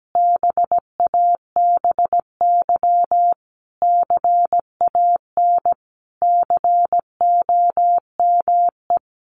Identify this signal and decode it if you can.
Morse code, heard as a keyed tone, about 17 wpm, 700 Hz. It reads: BABY CAN COME
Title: Sentences from Top 1300 Words 17wpm